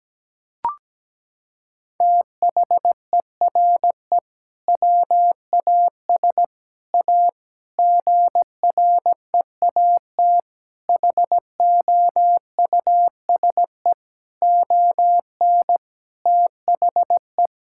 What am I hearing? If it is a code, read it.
Morse code, heard as a keyed tone, about 17 wpm, 700 Hz. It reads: THERE WAS A GREAT HOUSE ON THE
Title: Sentences from Top 1600 Words 17wpm